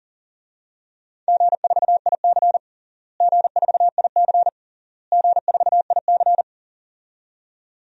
Morse code, heard as a keyed tone, about 40 wpm, 700 Hz. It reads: G4IC G4IC G4IC
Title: DXCC-100 Call Signs - 4 Characters - Repeated 3x - 40wpm